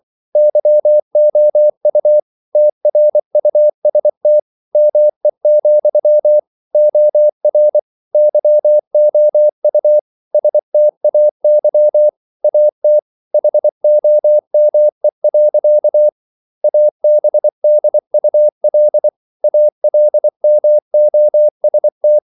Morse code, heard as a keyed tone, about 24 wpm, 600 Hz. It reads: YOU TRUST ME, OR YOU STAY AT HOME. ABDUL ALMOST